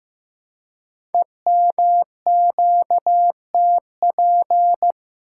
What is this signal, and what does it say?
Morse code, heard as a keyed tone, about 15 wpm, 700 Hz. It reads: EMQTP
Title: Code Groups - 5 Letters - 15wpm